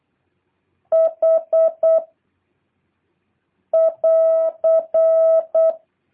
Morse code, heard as a keyed tone, about 8 wpm, 650 Hz. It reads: H <AR>
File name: jh8cbh / 2023cw